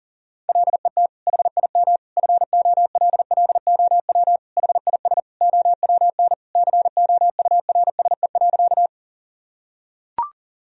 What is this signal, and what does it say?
Morse code, heard as a keyed tone, about 40 wpm, 700 Hz. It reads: LET HIM FOLLOW HIS OWN COURSE.
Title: Sentences from Top 1000 Words - Review Part 1 - 40wpm